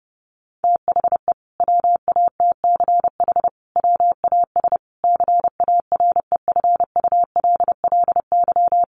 Morse code, heard as a keyed tone, about 30 wpm, 700 Hz. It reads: THE WATCH WAS CAREFULLY
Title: Sentences from Top 1300 Words 30wpm